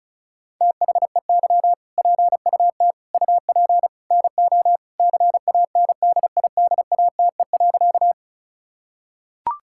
Morse code, heard as a keyed tone, about 34 wpm, 700 Hz. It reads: THEY PUT UP NO CANDIDATE.
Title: Sentences from Top 1600 Words 35wpm